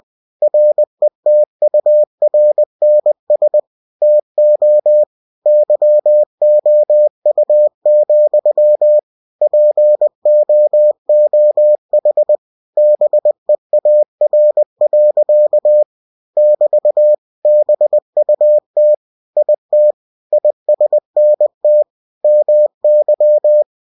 Morse code, heard as a keyed tone, about 20 wpm, 600 Hz. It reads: RETURNS TO YOU, POOH BEAR. <BT> BUT IT ISNT MY
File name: cw-books / winnie-the-pooh / winnie-20wpm-page-0047